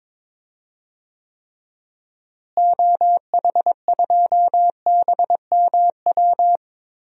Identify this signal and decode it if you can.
Morse code, heard as a keyed tone, about 22 wpm, 700 Hz. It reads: OH2BMW
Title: DXCC-100 Call Signs - 6 Characters - Repeated 2x - 22wpm